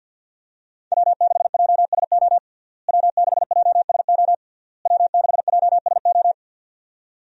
Morse code, heard as a keyed tone, about 51 wpm, 700 Hz. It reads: W6JSO W6JSO W6JSO
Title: USA Call Signs - 5 Characters - Repeated 3x - 50wpm